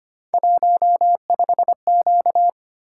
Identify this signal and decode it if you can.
Morse code, heard as a keyed tone, about 25 wpm, 700 Hz. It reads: 15Q
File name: Three Letter-Number Combinations - 25wpm